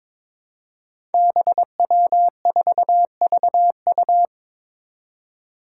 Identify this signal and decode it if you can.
Morse code, heard as a keyed tone, about 22 wpm, 700 Hz. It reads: BW4VU
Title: Five Challenging Letter-Number Combinations 22wpm